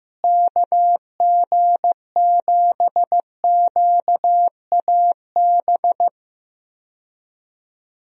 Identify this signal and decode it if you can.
Morse code, heard as a keyed tone, about 15 wpm, 700 Hz. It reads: KG7QAB
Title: USA Call Signs - 6 Characters - Repeated 3x - 15wpm